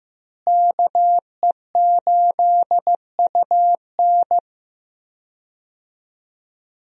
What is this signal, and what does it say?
Morse code, heard as a keyed tone, about 15 wpm, 700 Hz. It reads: KE8UN
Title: USA Call Signs - 5 Characters - Repeated 2x - 15wpm